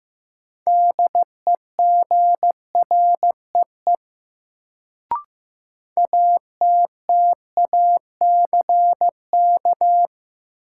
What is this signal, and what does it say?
Morse code, heard as a keyed tone, about 15 wpm, 700 Hz. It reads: DEGREE ATTACK
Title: Sentences from Top 1500 Words - New Word Drill - 15wpm